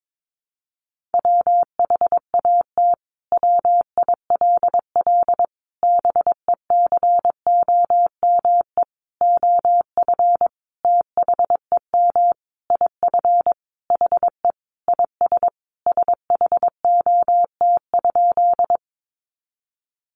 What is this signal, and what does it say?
Morse code, heard as a keyed tone, about 22 wpm, 700 Hz. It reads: WHAT WILL BECOME OF THEM IF HE IS SHOT?
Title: Sentences from Top 1100 Words 22wpm